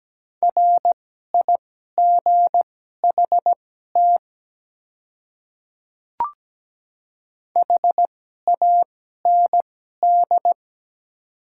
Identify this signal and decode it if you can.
Morse code, heard as a keyed tone, about 17 wpm, spaced as for 12 wpm, 700 Hz. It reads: RIGHT HAND
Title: 2x Character Spacing - Top 300 Words - 17wpm